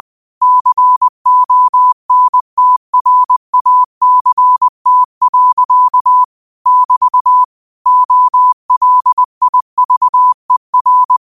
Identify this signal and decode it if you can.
Morse code, heard as a keyed tone, about 20 wpm, 1000 Hz. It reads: CONTRACT. <BT> OLIVER
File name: qotdfiles / 20wpm